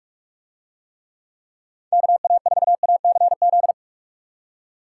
Morse code, heard as a keyed tone, about 43 wpm, 700 Hz. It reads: KA4ACZ